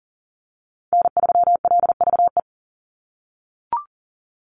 Morse code, heard as a keyed tone, about 40 wpm, 700 Hz. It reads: N3LVE
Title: USA Call Signs 40wpm